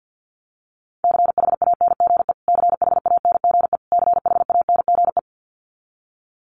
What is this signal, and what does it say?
Morse code, heard as a keyed tone, about 50 wpm, 700 Hz. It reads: CHANGE CHANGE CHANGE
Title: Top 300 Words Repeated 50wpm